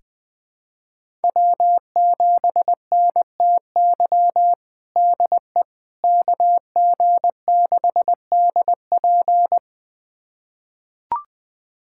Morse code, heard as a keyed tone, about 20 wpm, 700 Hz. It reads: W7NTY DE KG6DP
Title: USA Callsign DE Callsign - 5 Characters - 20wpm